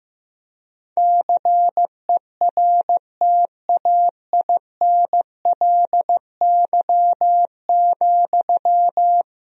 Morse code, heard as a keyed tone, about 15 wpm, 700 Hz. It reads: CERTAINLY,
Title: Sentences from Top 1000 Words - Review Part 2 - 15wpm